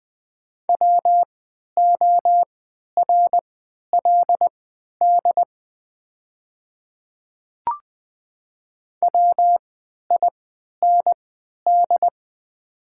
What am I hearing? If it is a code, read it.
Morse code, heard as a keyed tone, about 20 wpm, spaced as for 11 wpm, 700 Hz. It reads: WORLD WIND
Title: 3x Character Spacing - Top 500 Words - 20wpm